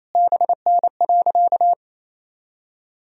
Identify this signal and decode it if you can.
Morse code, heard as a keyed tone, about 27 wpm, 700 Hz. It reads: BN.
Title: Sentences from Top 300 Words and Abbreviations 28wpm